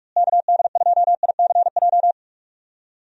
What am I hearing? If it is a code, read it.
Morse code, heard as a keyed tone, about 41 wpm, 700 Hz. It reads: KD2ICJ
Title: USA Call Signs - 6 Characters - Repeated 3x - 45wpm